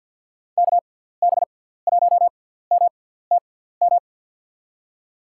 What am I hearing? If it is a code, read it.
Morse code, heard as a keyed tone, about 47 wpm, spaced as for 17 wpm, 700 Hz. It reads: KB1MTM
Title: USA Call Signs - 6x Character Spacing - 50wpm